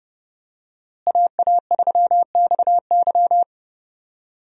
Morse code, heard as a keyed tone, about 32 wpm, 700 Hz. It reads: AA3XY
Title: USA Call Signs - 5 Characters - Repeated 2x - 30wpm